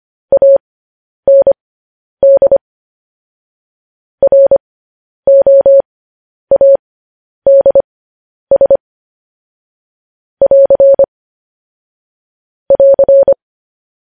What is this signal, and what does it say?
Morse code, heard as a keyed tone, about 25 wpm, spaced as for 10 wpm, 550 Hz. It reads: AND ROADS <AR> <AR>
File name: podcast / News headlines-1x-25x10